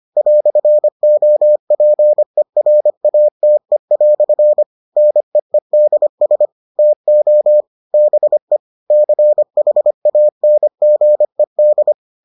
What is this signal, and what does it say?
Morse code, heard as a keyed tone, about 25 wpm, 600 Hz. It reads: "OPERATE" NEEDS TO BE CHANGED